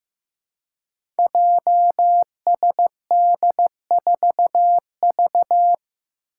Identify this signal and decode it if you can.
Morse code, heard as a keyed tone, about 15 wpm, 700 Hz. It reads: JSD4V
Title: Five Challenging Letter-Number Combinations 15wpm